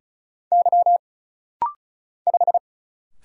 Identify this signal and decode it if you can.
Morse code, heard as a keyed tone, about 36 wpm, 700 Hz. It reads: Y 5
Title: Single Letter-Number - Rapid-Fire 35wpm